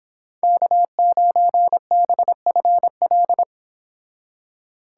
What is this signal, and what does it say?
Morse code, heard as a keyed tone, about 27 wpm, 700 Hz. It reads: K9BFL
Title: Speed Racing - USA 1x3 Call Signs 22wpm